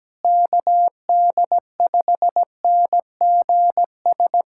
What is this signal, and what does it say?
Morse code, heard as a keyed tone, about 17 wpm, 700 Hz. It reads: KD5NGS
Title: USA Callsign DE Callsign 17wpm